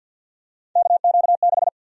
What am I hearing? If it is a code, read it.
Morse code, heard as a keyed tone, about 52 wpm, 700 Hz. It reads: KQ6
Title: USA Call Sign Prefixes - 50wpm